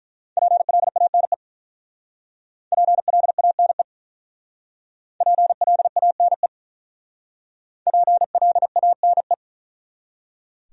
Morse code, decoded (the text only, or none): PLANE PLANE PLANE PLANE